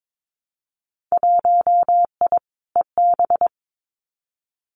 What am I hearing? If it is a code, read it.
Morse code, heard as a keyed tone, about 22 wpm, 700 Hz. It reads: 1I EB
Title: Winter Field Day Exchanges Encore 22wpm